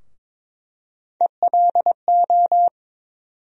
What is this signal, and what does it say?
Morse code, heard as a keyed tone, about 22 wpm, 700 Hz. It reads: ELO
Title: USA Call Sign Postfixes - 3 Letters 22wpm